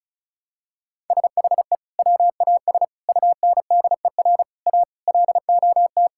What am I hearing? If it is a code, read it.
Morse code, heard as a keyed tone, about 34 wpm, 700 Hz. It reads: SHE WAS UNDER A LOT
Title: Sentences from Top 1000 Words - Review Part 6 - 35wpm